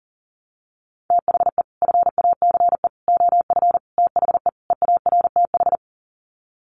Morse code, heard as a keyed tone, about 40 wpm, 700 Hz. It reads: THE FACE OF THE EARTH
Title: Sets of 5 Words Encore 40wpm